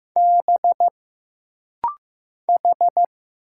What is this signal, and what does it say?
Morse code, heard as a keyed tone, about 15 wpm, 700 Hz. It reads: B H
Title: Single Letters - Rapid-Fire 15wpm